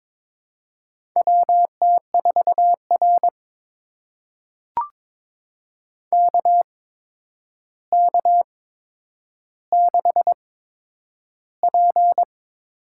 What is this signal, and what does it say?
Morse code, heard as a keyed tone, about 22 wpm, spaced as for 9 wpm, 700 Hz. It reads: WT4R KK6P
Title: USA Call Signs - 4 Characters - 8x Character Spacing - 22wpm